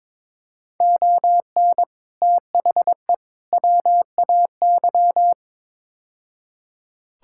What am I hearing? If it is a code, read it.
Morse code, heard as a keyed tone, about 22 wpm, 700 Hz. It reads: ON THE WAY
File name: practice / 22wpm / Build A Phrase - 5-Word Phrases - 22wpm